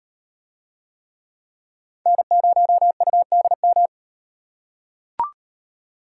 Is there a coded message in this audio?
Yes: Morse code from 2.1 to 3.9 s, heard as a keyed tone, about 38 wpm, 700 Hz. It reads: N0UDM